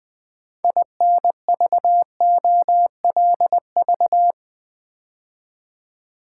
Morse code, heard as a keyed tone, about 20 wpm, 700 Hz. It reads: INVOLV
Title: Build A Word - 2 Letters At A Time - 8-Letter Words - 20wpm